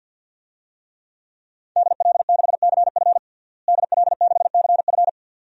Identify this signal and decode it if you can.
Morse code, heard as a keyed tone, about 52 wpm, 700 Hz. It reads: DL6CF DL6CF
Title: DXCC-100 Call Signs - 5 Characters - Repeated 2x - 50wpm